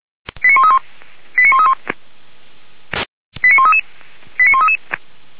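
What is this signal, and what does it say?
Five-tone selective call (ZVEI), heard as quick runs of five tones, tones of about 70 ms. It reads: 89131 89131 89133 89133
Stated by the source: Five-tone ZVEI selective call received by SDR, from the audio server of Feuerwehr Orlamünde (fire brigade).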